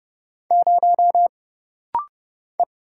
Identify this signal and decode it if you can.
Morse code, heard as a keyed tone, about 30 wpm, 700 Hz. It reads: 0 E
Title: Single Letter-Number - Rapid-Fire 30wpm